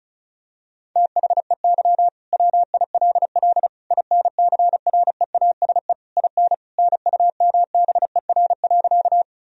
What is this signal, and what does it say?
Morse code, heard as a keyed tone, about 36 wpm, 700 Hz. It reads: THEY WILL INCREASE IN NUMBER.
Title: Sentences from Top 1000 Words - Review Part 4 - 35wpm